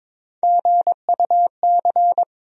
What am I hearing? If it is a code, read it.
Morse code, heard as a keyed tone, about 22 wpm, 700 Hz. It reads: GUC
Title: USA Call Sign Postfixes - 3 Letters 22wpm